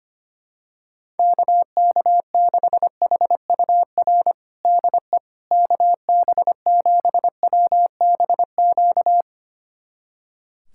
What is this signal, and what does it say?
Morse code, heard as a keyed tone, about 25 wpm, 700 Hz. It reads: KK6HUR DE KB7WBQ
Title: USA Callsign DE Callsign - 6 Characters - 25wpm